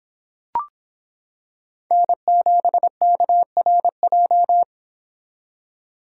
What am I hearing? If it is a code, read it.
Morse code, heard as a keyed tone, about 26 wpm, 700 Hz. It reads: N7KRJ